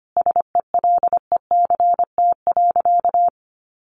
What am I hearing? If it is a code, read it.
Morse code, heard as a keyed tone, about 25 wpm, 700 Hz. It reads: SELECT.
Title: Sentences from Top 1300 Words 25wpm